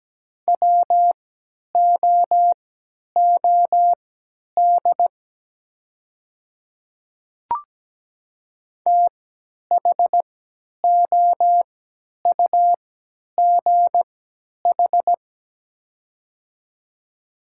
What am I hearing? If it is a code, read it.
Morse code, heard as a keyed tone, about 17 wpm, spaced as for 10 wpm, 700 Hz. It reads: WOOD THOUGH
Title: 3x Character Spacing - Top 500 Words - 17wpm